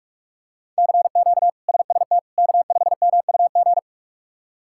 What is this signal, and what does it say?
Morse code, heard as a keyed tone, about 42 wpm, 700 Hz. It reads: CQ SST K5MUG
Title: K1USN SST QSO Practice - 45wpm